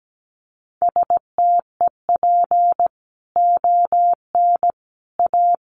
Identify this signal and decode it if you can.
Morse code, heard as a keyed tone, about 17 wpm, 700 Hz. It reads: STEP ON A
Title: A Sound Of Thunder 17wpm